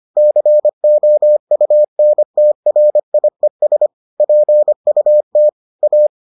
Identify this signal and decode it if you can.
Morse code, heard as a keyed tone, about 25 wpm, 600 Hz. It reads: COUNTRIES PUT A